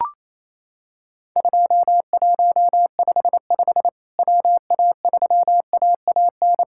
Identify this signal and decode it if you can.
Morse code, heard as a keyed tone, about 28 wpm, 700 Hz. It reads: 2155 WA3AAN
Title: Worked All Europe DX Contest - QTC Exchange - 28wpm